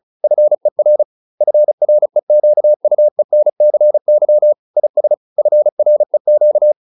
Morse code, heard as a keyed tone, about 34 wpm, 600 Hz. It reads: FER FREQUENCY IS FREQ